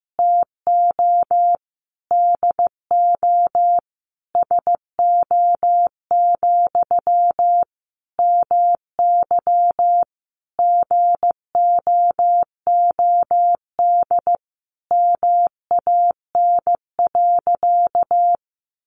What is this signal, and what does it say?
Morse code, heard as a keyed tone, about 15 wpm, 700 Hz. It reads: TO DO SO, MY GOOD MAN.
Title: Sentences from Top 1200 Words 15wpm